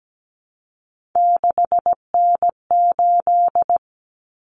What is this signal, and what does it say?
Morse code, heard as a keyed tone, about 17 wpm, 700 Hz. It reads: 6N8
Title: Common Cut Numbers - 3-Digit 17wpm